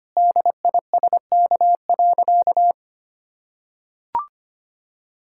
Morse code, heard as a keyed tone, about 25 wpm, 700 Hz. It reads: DISK.
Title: Sentences from Top 1000 Words - Review Part 6 - 25wpm